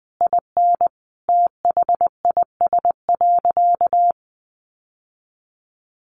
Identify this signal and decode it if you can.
Morse code, heard as a keyed tone, about 20 wpm, 700 Hz. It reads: IN THIS.
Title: Sentences from Top 1300 Words 20wpm